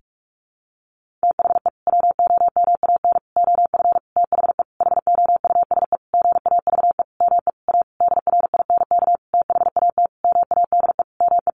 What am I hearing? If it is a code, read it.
Morse code, heard as a keyed tone, about 44 wpm, 700 Hz. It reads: THE WOMAN OF THE HOUSE GAVE ME A DRINK THAT MADE ME
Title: Sentences from Top 1500 Words 45wpm